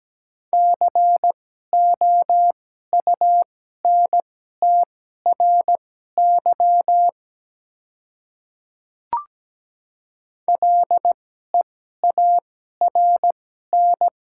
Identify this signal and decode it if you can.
Morse code, heard as a keyed tone, about 17 wpm, spaced as for 12 wpm, 700 Hz. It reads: COUNTRY LEARN
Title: 2x Character Spacing - Top 300 Words - 17wpm